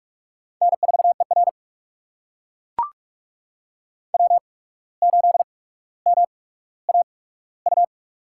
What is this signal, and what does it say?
Morse code, heard as a keyed tone, about 43 wpm, spaced as for 18 wpm, 700 Hz. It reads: N4ER W8MAU
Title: USA Call Signs - 8x Character Spacing - 45wpm